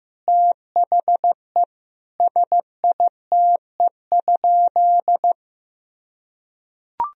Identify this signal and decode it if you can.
Morse code, heard as a keyed tone, about 15 wpm, 700 Hz. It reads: THE SITE?
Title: Sentences from Top 1000 Words - Review Part 4 - 15wpm